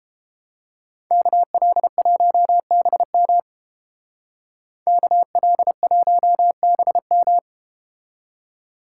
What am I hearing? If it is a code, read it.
Morse code, heard as a keyed tone, about 31 wpm, 700 Hz. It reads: KL1BM KL1BM